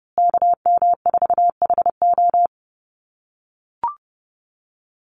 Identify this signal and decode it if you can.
Morse code, heard as a keyed tone, about 30 wpm, 700 Hz. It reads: KM4HO
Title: USA Call Signs Repeated 3x 30wpm